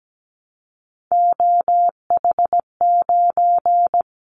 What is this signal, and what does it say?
Morse code, heard as a keyed tone, about 17 wpm, 700 Hz. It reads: OH9
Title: Callsign Prefixes for Top 25 DXCC - Repeated 3x - 17wpm